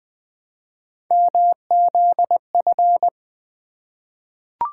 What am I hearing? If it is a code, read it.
Morse code, heard as a keyed tone, about 20 wpm, 700 Hz. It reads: MZF
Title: Three Letter-Number Combinations - 20wpm